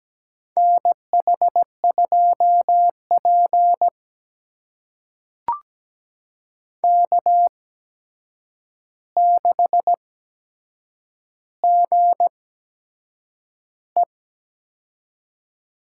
Morse code, heard as a keyed tone, about 17 wpm, spaced as for 7 wpm, 700 Hz. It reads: NH2P K6GE
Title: USA Call Signs - 4 Characters - 8x Character Spacing - 17wpm